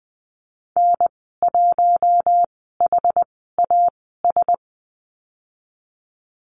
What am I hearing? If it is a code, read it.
Morse code, heard as a keyed tone, about 20 wpm, spaced as for 14 wpm, 700 Hz. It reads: N1HAS
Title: 069 Lesson 12g - Letter d - Call Signs 20wpm